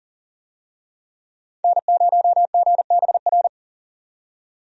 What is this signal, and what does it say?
Morse code, heard as a keyed tone, about 37 wpm, 700 Hz. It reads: N0GBR
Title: USA Call Signs - 5 Characters - 40wpm